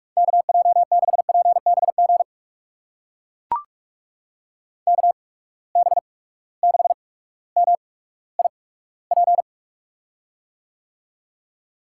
Morse code, heard as a keyed tone, about 43 wpm, spaced as for 19 wpm, 700 Hz. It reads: KJ6PBG KB6MIP
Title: USA Call Signs - 6 Characters - 8x Character Spacing - 45wpm